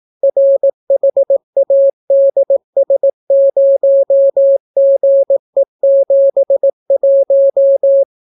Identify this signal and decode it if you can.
Morse code, heard as a keyed tone, about 18 wpm, 550 Hz. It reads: RHADS0GE71